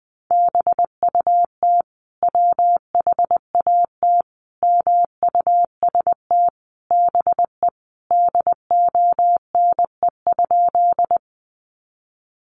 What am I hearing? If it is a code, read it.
Morse code, heard as a keyed tone, about 20 wpm, 700 Hz. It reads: BUT WHAT MUST BE DONE?
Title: Sentences from Top 500 Words 20wpm